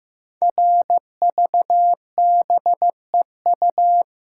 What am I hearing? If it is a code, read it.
Morse code, heard as a keyed tone, about 15 wpm, 700 Hz. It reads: RVBEU
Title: Code Groups - 5 Letters - 15wpm